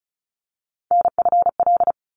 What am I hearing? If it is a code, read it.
Morse code, heard as a keyed tone, about 34 wpm, 700 Hz. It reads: NFL
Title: ARRL Sections 35wpm